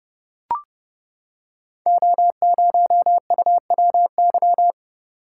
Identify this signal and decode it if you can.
Morse code, heard as a keyed tone, about 30 wpm, 700 Hz. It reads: O0UWY
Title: Code Groups - 5 Letter:Numbers - 30wpm